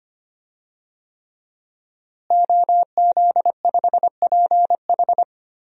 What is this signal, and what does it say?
Morse code, heard as a keyed tone, about 25 wpm, 700 Hz. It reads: OZ5PH
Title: DXCC-100 Call Signs - 5 Characters - Repeated 3x - 25wpm